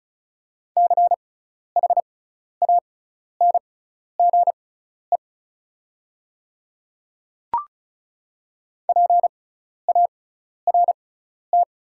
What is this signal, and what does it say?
Morse code, heard as a keyed tone, about 36 wpm, spaced as for 12 wpm, 690 Hz. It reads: CHANGE PART
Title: 6x Character Spacing - Top 300 Words - 35wpm